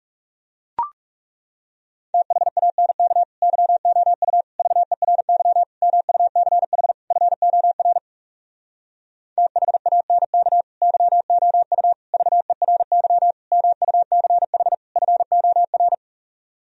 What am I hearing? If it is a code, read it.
Morse code, heard as a keyed tone, about 40 wpm, 700 Hz. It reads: THANK YOU VERY MUCH FOR THANK YOU VERY MUCH FOR